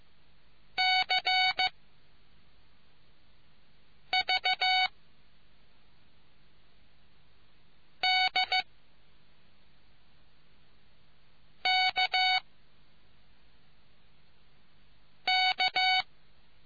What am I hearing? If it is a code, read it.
Morse code, heard as a keyed tone, about 15 wpm, spaced as for 3 wpm, 750 Hz. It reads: CVDKK